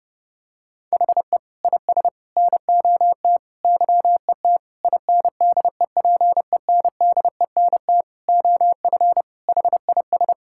Morse code, heard as a keyed tone, about 30 wpm, 700 Hz. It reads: HE IS NOT YET INDEPENDENT OF HIS
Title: Sentences from Top 1000 Words - Review Part 6 - 30wpm